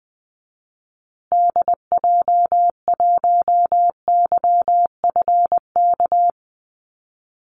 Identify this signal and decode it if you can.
Morse code, heard as a keyed tone, about 20 wpm, 700 Hz. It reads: DJ1YFK
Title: Call Signs Repeated 2x - 20wpm